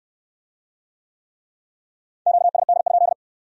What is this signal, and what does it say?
Morse code, heard as a keyed tone, about 64 wpm, 700 Hz. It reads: QSD?